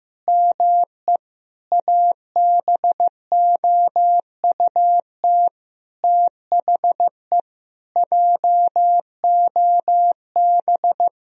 Morse code, heard as a keyed tone, about 15 wpm, 700 Hz. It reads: ME ABOUT THE JOB
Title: Sentences from Top 1000 Words - Review Part 3 - 15wpm